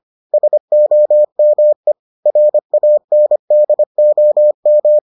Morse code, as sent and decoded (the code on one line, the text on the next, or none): ... --- -- . / .-. .- -. -.. --- --
SOME RANDOM